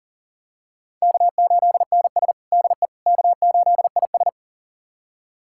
Morse code, heard as a keyed tone, about 37 wpm, 700 Hz. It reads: K8NS DE K8IS
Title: USA Callsign DE Callsign - 4 Characters - 40wpm